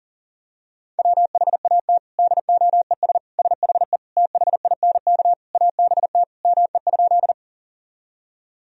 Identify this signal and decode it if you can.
Morse code, heard as a keyed tone, about 37 wpm, 700 Hz. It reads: WHAT DOES SHE THINK ABT ME?
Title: Sentences from Top 200 Words and Abbreviations 40wpm